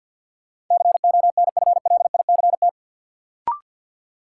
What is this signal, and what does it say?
Morse code, heard as a keyed tone, about 53 wpm, 700 Hz. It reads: CONFLICT
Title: Sentences from Top 1300 Words - New Word Drill Alt - 50wpm